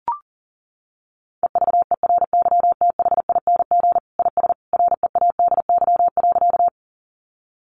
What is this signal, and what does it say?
Morse code, heard as a keyed tone, about 40 wpm, 700 Hz. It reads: EVERYTHING IS READY.
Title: Sentences from Top 700 Words 40wpm